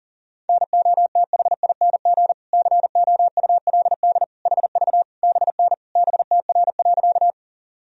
Morse code, heard as a keyed tone, about 40 wpm, 700 Hz. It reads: NOTHING COULD HV BN BTR.
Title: Sentences from Top 500 Words and Abbreviations 40wpm